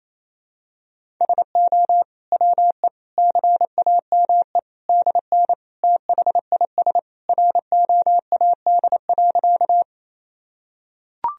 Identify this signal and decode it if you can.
Morse code, heard as a keyed tone, about 27 wpm, 700 Hz. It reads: SO WE CAME DN THIS ROAD.
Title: Sentences from Top 400 Words and Abbreviations 28wpm